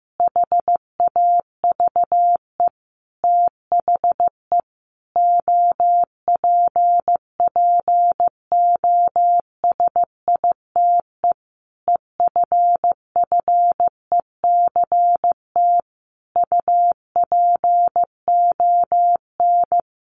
Morse code, heard as a keyed tone, about 15 wpm, 700 Hz. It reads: HAVE THE OPPOSITE EFFECT UPON
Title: Sentences from Top 1300 Words 15wpm